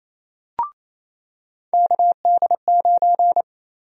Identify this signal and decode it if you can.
Morse code, heard as a keyed tone, about 28 wpm, 700 Hz. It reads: KD9
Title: USA Call Sign Prefixes - Repeated 2x - 28wpm